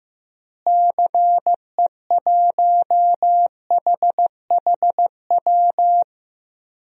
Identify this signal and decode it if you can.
Morse code, heard as a keyed tone, about 15 wpm, 700 Hz. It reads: CE1HHW
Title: DXCC-100 Call Signs - 6 Characters - Repeated 3x - 15wpm